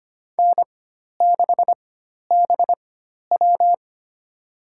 Morse code, heard as a keyed tone, about 25 wpm, spaced as for 12 wpm, 700 Hz. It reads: N6BW
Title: USA Call Signs - 4 Characters - 4x Character Spacing - 25wpm